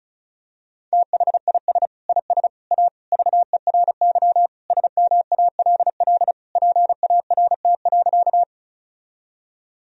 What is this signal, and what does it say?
Morse code, heard as a keyed tone, about 36 wpm, 700 Hz. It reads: THIS IS A VERY SMALL PART.
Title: Sentences from Top 1000 Words - Review Part 1 - 35wpm